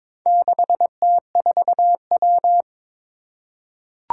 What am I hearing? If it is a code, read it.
Morse code, heard as a keyed tone, about 22 wpm, 700 Hz. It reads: 6T4W
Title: Four Challenging Letter-Number Combinations 22wpm